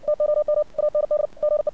Morse code, heard as a keyed tone, about 56 wpm, 600 Hz. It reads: TOM AND GE
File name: archived / CRSnet